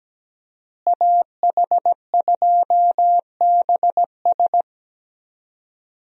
Morse code, heard as a keyed tone, about 17 wpm, 700 Hz. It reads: AH2BS
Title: USA Call Signs - 5 Characters - Repeated 3x - 17wpm